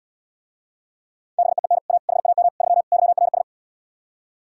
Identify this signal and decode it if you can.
Morse code, heard as a keyed tone, about 78 wpm, 700 Hz. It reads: BEEN A HAM 3 YRS